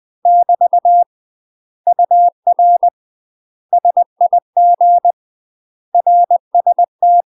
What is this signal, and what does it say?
Morse code, heard as a keyed tone, about 20 wpm, 700 Hz. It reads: <BT> UR SIG RST
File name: UK MASTER QSOs TRAINING / 20WPM / UK MASTER QSOs 0063